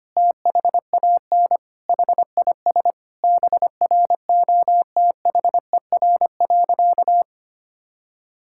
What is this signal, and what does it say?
Morse code, heard as a keyed tone, about 25 wpm, 700 Hz. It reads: THAN HIS BROTHER.
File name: practice / 25wpm / Difficult Sentences with Hints - 25wpm